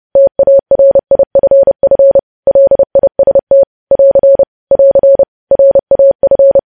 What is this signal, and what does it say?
Morse code, heard as a keyed tone, about 30 wpm, 550 Hz. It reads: TARIFF LIST <AR> <AR> RAF
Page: news headlines in morse code at 30 wpm